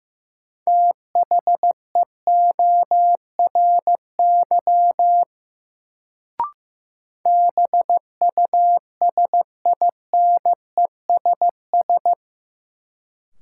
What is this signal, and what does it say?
Morse code, heard as a keyed tone, about 15 wpm, 700 Hz. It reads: THEORY BUSINESS
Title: Sentences from Top 600 Words - New Word Drill - 15wpm